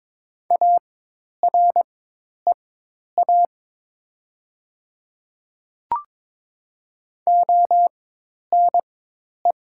Morse code, heard as a keyed tone, about 22 wpm, spaced as for 10 wpm, 700 Hz. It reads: AREA ONE
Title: 4x Character Spacing - Top 500 Words - 22wpm